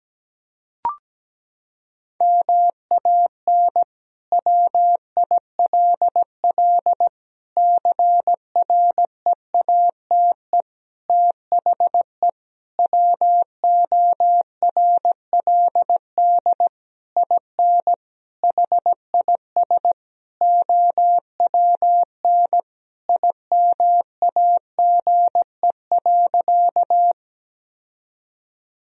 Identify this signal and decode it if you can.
Morse code, heard as a keyed tone, about 17 wpm, 700 Hz. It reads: MAN WILL CREATE THE WORLD IN HIS OWN IMAGE.
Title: Sentences from Top 1000 Words - Review Part 4 - 17wpm